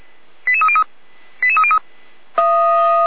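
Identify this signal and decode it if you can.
Five-tone selective call (ZVEI), heard as quick runs of five tones, tones of about 70 ms. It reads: 88303 88303
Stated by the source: ZVEI five-tone call received by SDR, archived by Feuerwehr Orlamünde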